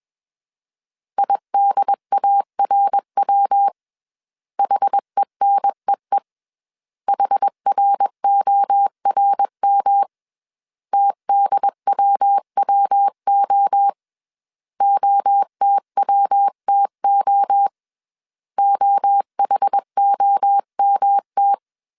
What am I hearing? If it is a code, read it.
Morse code, heard as a keyed tone, about 21 wpm, 800 Hz. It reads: IDARW HENEE HRORM TDWWO OTWTO OHOMT